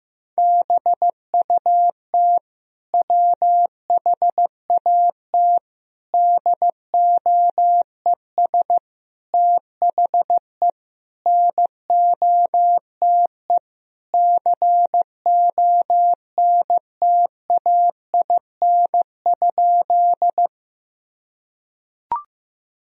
Morse code, heard as a keyed tone, about 15 wpm, 700 Hz. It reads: BUT WHAT DOES THE NOTE CONTAIN?
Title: Sentences from Top 1000 Words - Review Part 2 - 15wpm